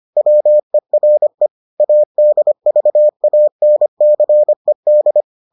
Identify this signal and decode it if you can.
Morse code, heard as a keyed tone, about 25 wpm, 600 Hz. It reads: WERE ADVANCED